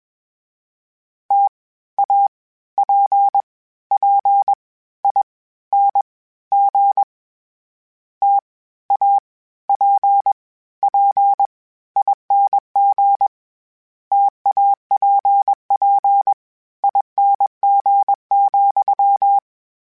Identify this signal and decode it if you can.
Morse code, heard as a keyed tone, about 21 wpm, 800 Hz. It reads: T A P P I N G T A P P ING TAPP ING,
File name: Variable Spacing / Variable Spacing-mp3s / T AOEs @20